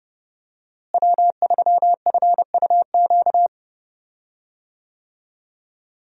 Morse code, heard as a keyed tone, about 31 wpm, 700 Hz. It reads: W3FUQ
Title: USA Call Signs - 5 Characters - Repeated 2x - 30wpm